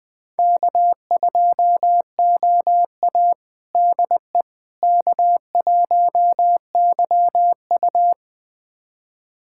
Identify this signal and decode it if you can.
Morse code, heard as a keyed tone, about 20 wpm, 700 Hz. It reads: K2OA DE K1YU
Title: USA Callsign DE Callsign - 4 Characters - 20wpm